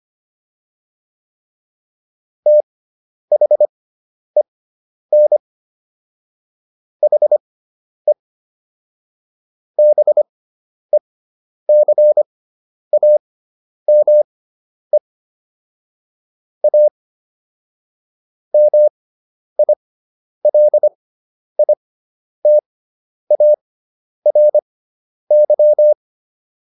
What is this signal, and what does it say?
Morse code, heard as a keyed tone, about 25 wpm, spaced as for 10 wpm, 600 Hz. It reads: THEN HE BECAME A MILITARY